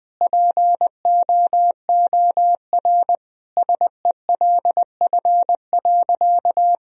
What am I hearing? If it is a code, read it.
Morse code, heard as a keyed tone, about 20 wpm, 700 Hz. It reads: POOR SELF.
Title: Sentences from Top 1000 Words - Review Part 1 - 20wpm